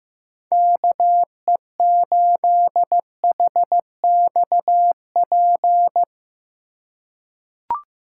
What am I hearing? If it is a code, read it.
Morse code, heard as a keyed tone, about 15 wpm, 700 Hz. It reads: KE8HXP
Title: USA Call Signs - 6 Characters - 6x Character Spacing - 15wpm